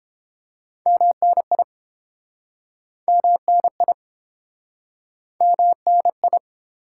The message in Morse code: MNI MNI MNI

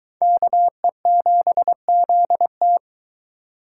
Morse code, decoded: KE7ZT

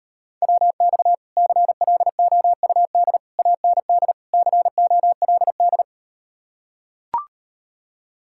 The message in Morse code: WX CLOUD AND COLD

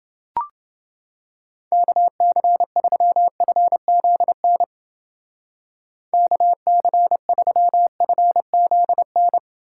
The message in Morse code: KC3FZN KC3FZN

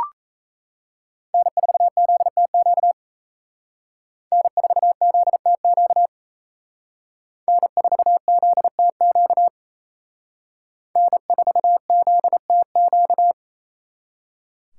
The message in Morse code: N4ZTQ N4ZTQ N4ZTQ N4ZTQ

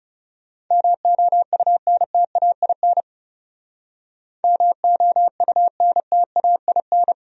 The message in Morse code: MOUNTAIN MOUNTAIN

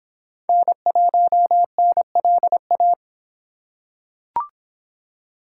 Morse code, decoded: N1NLA

